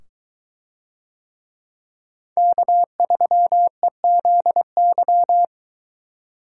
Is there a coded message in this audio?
Yes: Morse code from 2.4 to 5.4 s, heard as a keyed tone, about 22 wpm, 700 Hz. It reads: K3EZY